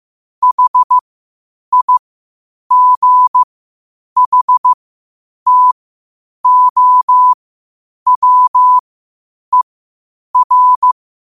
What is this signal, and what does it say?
Morse code, heard as a keyed tone, about 15 wpm, spaced as for 8 wpm, 1000 Hz. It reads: HIGHTOWER
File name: qotdfiles / 8wpm